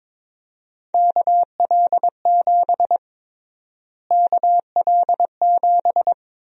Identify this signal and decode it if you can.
Morse code, heard as a keyed tone, about 21 wpm, 700 Hz. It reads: KL7 KL7
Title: USA Call Sign Prefixes - Repeated 2x - 22wpm